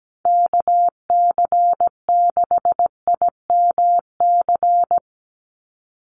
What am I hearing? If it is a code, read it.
Morse code, heard as a keyed tone, about 17 wpm, 700 Hz. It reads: KC6IMC
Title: USA Call Signs 17wpm